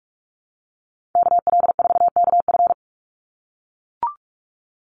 Morse code, heard as a keyed tone, about 44 wpm, 700 Hz. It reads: KL4KF
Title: USA Call Signs 45wpm